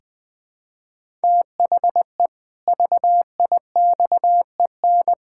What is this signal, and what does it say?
Morse code, heard as a keyed tone, about 19 wpm, 700 Hz. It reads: THE VIXEN
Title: Pangram Challenge v2 - 20wpm